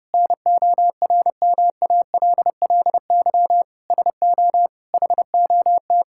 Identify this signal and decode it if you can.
Morse code, heard as a keyed tone, about 30 wpm, 700 Hz. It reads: NORMALLY SO HOT
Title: Sentences from Top 1000 Words - Review Part 6 - 30wpm